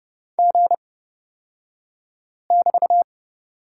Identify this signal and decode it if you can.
Morse code, heard as a keyed tone, about 29 wpm, spaced as for 5 wpm, 690 Hz. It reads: G<BT>
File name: Lesson 36d - BT - Two Characters 30wpm